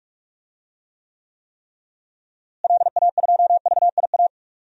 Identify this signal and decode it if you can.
Morse code, heard as a keyed tone, about 46 wpm, 700 Hz. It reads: LA2VIA
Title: DXCC-100 Call Signs - 6 Characters - Repeated 3x - 45wpm